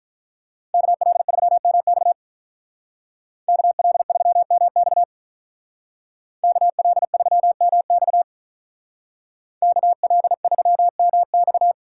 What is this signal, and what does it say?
Morse code, heard as a keyed tone, about 43 wpm, 700 Hz. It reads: KL3MX KL3MX KL3MX KL3MX
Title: Speed Racing - USA 2x2 Call Signs 35wpm